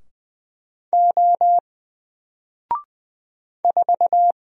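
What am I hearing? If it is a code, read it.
Morse code, heard as a keyed tone, about 20 wpm, 700 Hz. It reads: O 4
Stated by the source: Single letter and number Morse drill at 20 wpm, from N4LCD.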